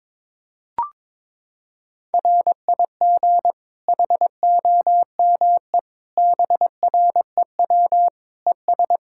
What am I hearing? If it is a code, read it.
Morse code, heard as a keyed tone, about 22 wpm, 700 Hz. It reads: RIG HOME BREW ES
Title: Practice QSO - Exchange 2 - Info - 22wpm